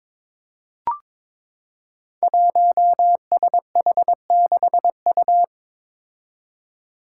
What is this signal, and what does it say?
Morse code, heard as a keyed tone, about 21 wpm, 700 Hz. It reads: 1SH6U
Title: Five Challenging Letter-Number Combinations 22wpm